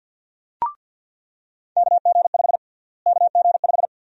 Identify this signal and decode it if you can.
Morse code, heard as a keyed tone, about 48 wpm, 700 Hz. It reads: KG5 KG5
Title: USA Call Sign Prefixes - Repeated 2x - 50wpm